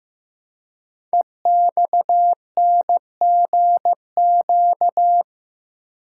Morse code, heard as a keyed tone, about 15 wpm, 700 Hz. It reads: EXNGQ
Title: Code Groups - 5 Letters - 15wpm